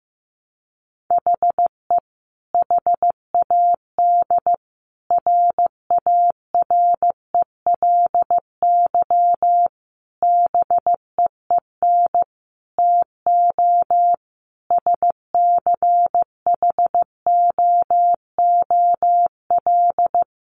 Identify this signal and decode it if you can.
Morse code, heard as a keyed tone, about 15 wpm, 700 Hz. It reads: HE HAD RARELY BEEN TO SCHOOL
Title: Sentences from Top 1400 Words 15wpm